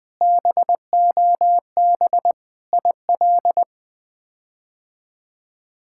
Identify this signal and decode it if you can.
Morse code, heard as a keyed tone, about 20 wpm, 700 Hz. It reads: BOB IL
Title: K1USN SST QSO Practice - 20wpm